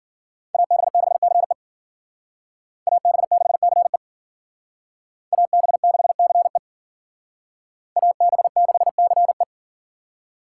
Morse code, heard as a keyed tone, about 52 wpm, 700 Hz. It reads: AB6CE AB6CE AB6CE AB6CE